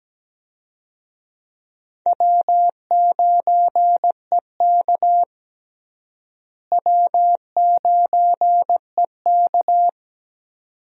Morse code, heard as a keyed tone, about 17 wpm, 700 Hz. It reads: W9EK W9EK
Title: USA Call Signs - 4 Characters - Repeated 2x - 17wpm